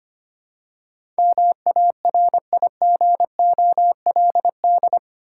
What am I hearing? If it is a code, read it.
Morse code, heard as a keyed tone, about 25 wpm, 700 Hz. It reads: MARIGOLD
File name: practice / 25wpm / Difficult Words with Hints - 25wpm